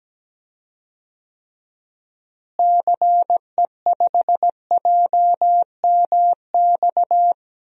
Morse code, heard as a keyed tone, about 17 wpm, 700 Hz. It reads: CE5JMX
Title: DXCC-100 Call Signs - 6 Characters - Repeated 2x - 17wpm